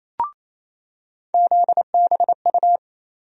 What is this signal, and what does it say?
Morse code, heard as a keyed tone, about 28 wpm, 700 Hz. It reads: ZBU
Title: USA Call Sign Postfixes - 3 Letters 28wpm